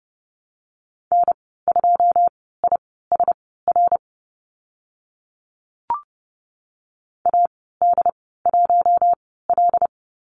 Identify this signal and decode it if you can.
Morse code, heard as a keyed tone, about 30 wpm, spaced as for 17 wpm, 700 Hz. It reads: N2ISR AD1L
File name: Lesson 14g - Number 2 - Call Signs 30wpm